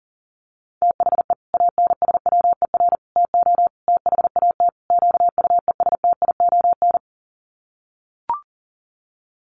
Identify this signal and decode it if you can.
Morse code, heard as a keyed tone, about 40 wpm, 700 Hz. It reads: THE ANSWER TO THAT QUESTION
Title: Sets of 5 Words Encore2 40wpm